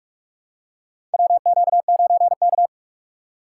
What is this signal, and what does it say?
Morse code, heard as a keyed tone, about 44 wpm, 700 Hz. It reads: WQ9K
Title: USA Call Signs - 4 Characters - 2x Character Spacing - 45wpm